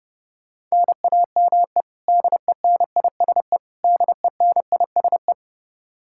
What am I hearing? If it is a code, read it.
Morse code, heard as a keyed tone, about 32 wpm, 700 Hz. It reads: NAME DENISE DENISE
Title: Practice QSO - Exchange 1 - Beginning - 30wpm